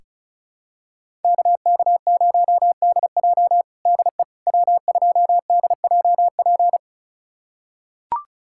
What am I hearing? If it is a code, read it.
Morse code, heard as a keyed tone, about 36 wpm, 700 Hz. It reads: KK0DJ DE W2DJP
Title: USA Callsign DE Callsign - 5 Characters - 35wpm